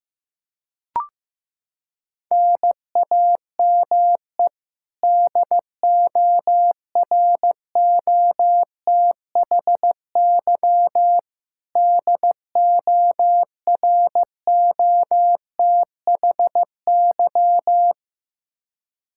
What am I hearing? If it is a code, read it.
Morse code, heard as a keyed tone, about 15 wpm, 700 Hz. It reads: NAME DOROTHY DOROTHY
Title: Practice QSO - Exchange 1 - Beginning - 15wpm